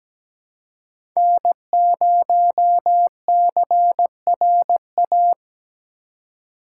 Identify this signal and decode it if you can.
Morse code, heard as a keyed tone, about 17 wpm, 700 Hz. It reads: N0CRA
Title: USA Call Signs - 3x Character Spacing - 17wpm